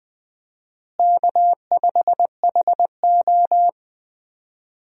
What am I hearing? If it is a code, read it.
Morse code, heard as a keyed tone, about 20 wpm, 700 Hz. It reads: K5HO
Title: USA Call Signs - 4 Characters - Repeated 3x - 20wpm